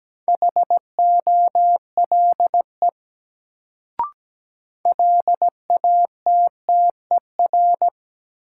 Morse code, heard as a keyed tone, about 17 wpm, 700 Hz. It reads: HOLE LATTER
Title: Sentences from Top 1600 Words - New Word Drill Alt - 17wpm